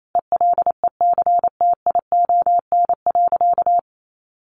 Morse code, heard as a keyed tone, about 28 wpm, 700 Hz. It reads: ELECTION.
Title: Sentences from Top 1500 Words 28wpm